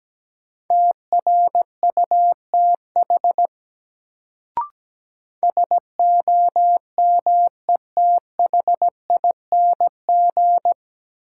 Morse code, heard as a keyed tone, about 17 wpm, 700 Hz. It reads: TRUTH SOMETHING
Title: Sets of 5 Words - New Word Drill - 17wpm